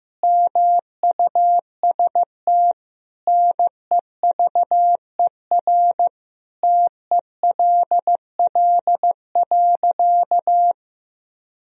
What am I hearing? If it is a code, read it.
Morse code, heard as a keyed tone, about 16 wpm, 700 Hz. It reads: MUST NEVER TELL.
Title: Sentences from Top 300 Words and Abbreviations 15wpm